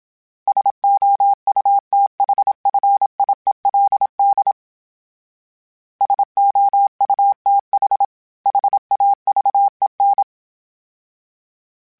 Morse code, heard as a keyed tone, about 27 wpm, 800 Hz. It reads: SOUTHFIELD SOUTH HAVEN
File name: fromSMs / FromSMs mp3s / Cities@25